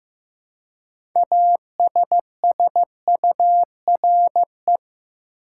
Morse code, heard as a keyed tone, about 15 wpm, 700 Hz. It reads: ASSURE
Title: Six Letter Words - 15wpm